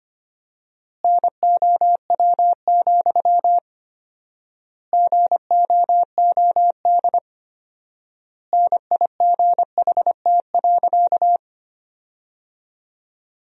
Morse code, heard as a keyed tone, about 25 wpm, 700 Hz. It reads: NOW, GOOD NIGHT.
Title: 4x Word-Spacing - Sentences from Top 300 Words - 25wpm